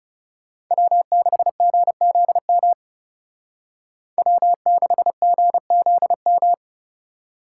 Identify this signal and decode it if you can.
Morse code, heard as a keyed tone, about 34 wpm, 700 Hz. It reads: W6GZM W6GZM
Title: Speed Racing - USA 1x3 Call Signs 30wpm